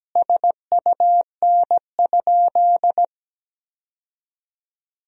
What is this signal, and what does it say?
Morse code, heard as a keyed tone, about 17 wpm, 700 Hz. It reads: SUN?
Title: Sentences from Top 1000 Words - Review Part 1 - 17wpm